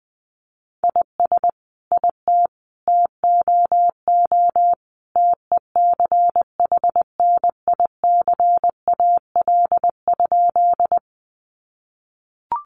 Morse code, heard as a keyed tone, about 20 wpm, 700 Hz. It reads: IS IT TOO TECHNICAL?
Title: Sentences from Top 1100 Words 20wpm